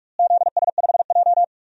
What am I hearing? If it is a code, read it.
Morse code, heard as a keyed tone, about 46 wpm, 700 Hz. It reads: ZS5J
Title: DXCC-100 Call Signs - 4 Characters - Repeated 3x - 45wpm